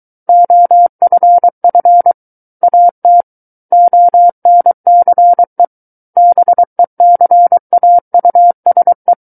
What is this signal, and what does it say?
Morse code, heard as a keyed tone, about 23 wpm, 700 Hz. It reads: OFF AT ONCE BECAUSE
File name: Book One / 14-Chapter 14 - 23 WPM